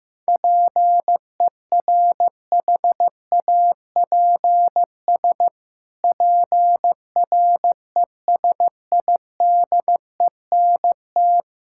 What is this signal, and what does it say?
Morse code, heard as a keyed tone, about 15 wpm, 700 Hz. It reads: PERHAPS PRESIDENT
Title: Pangram Challenge v2 - 15wpm